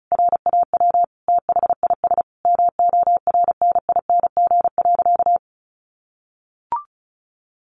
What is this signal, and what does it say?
Morse code, heard as a keyed tone, about 34 wpm, 700 Hz. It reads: RAW THIS MORNING.
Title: Sentences from Top 1500 Words 35wpm